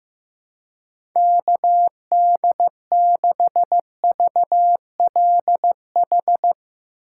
Morse code, heard as a keyed tone, about 15 wpm, 700 Hz. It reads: KD6VLH
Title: USA Call Signs - 6 Characters - 2x Character Spacing - 15wpm